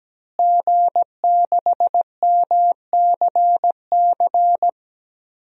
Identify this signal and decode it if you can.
Morse code, heard as a keyed tone, about 17 wpm, 700 Hz. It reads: G6MCC
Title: DXCC-100 Call Signs - 5 Characters - Repeated 2x - 17wpm